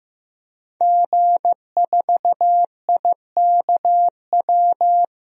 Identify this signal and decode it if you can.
Morse code, heard as a keyed tone, about 15 wpm, 700 Hz. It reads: G4IKW
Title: DXCC-100 Call Signs - 5 Characters - 15wpm